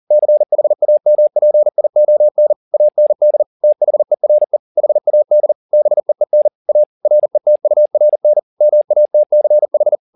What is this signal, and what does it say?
Morse code, heard as a keyed tone, about 40 wpm, 600 Hz. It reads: CHAMPION AND THERE HAD BEEN A RETURN MATCH
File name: the old man and the sea 2 40wpm